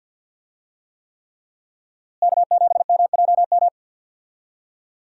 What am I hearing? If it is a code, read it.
Morse code, heard as a keyed tone, about 49 wpm, 700 Hz. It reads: K7MJM
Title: USA Call Signs - 5 Characters - Repeated 3x - 50wpm